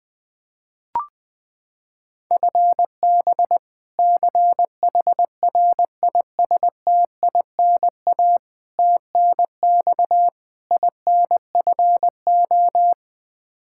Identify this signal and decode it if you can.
Morse code, heard as a keyed tone, about 20 wpm, 700 Hz. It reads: FB CHRISTINA TNX INFO
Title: Practice QSO - Exchange 3 - More Info - 20wpm